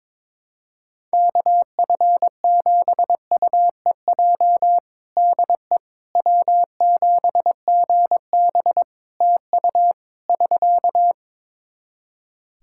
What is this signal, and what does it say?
Morse code, heard as a keyed tone, about 22 wpm, 700 Hz. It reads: KF7UEJ DE W7GB TU <SK>
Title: Practice QSO - Exchange 4 - Ending - 22wpm